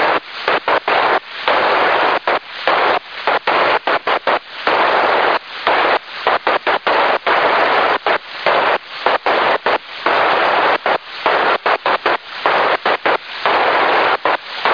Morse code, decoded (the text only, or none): none